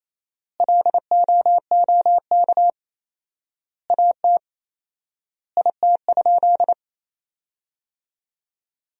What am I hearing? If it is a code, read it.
Morse code, heard as a keyed tone, about 28 wpm, 700 Hz. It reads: LOOK AT IT?
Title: 4x Word-Spacing - Sentences from Top 100 Words - 28wpm